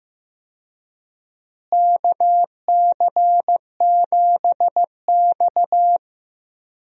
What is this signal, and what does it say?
Morse code, heard as a keyed tone, about 15 wpm, 700 Hz. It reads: KC7X
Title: USA Call Signs - 4 Characters - Repeated 3x - 15wpm